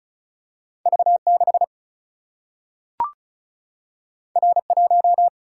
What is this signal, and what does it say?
Morse code, heard as a keyed tone, about 36 wpm, 700 Hz. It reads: V6 R1
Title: Single-Letter Single-Number 35wpm